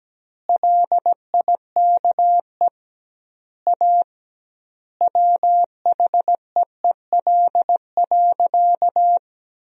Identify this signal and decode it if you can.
Morse code, heard as a keyed tone, about 17 wpm, 700 Hz. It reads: LIKE A WHEEL.
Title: 2x Word-Spacing - Sentences from Top 500 Words - 17wpm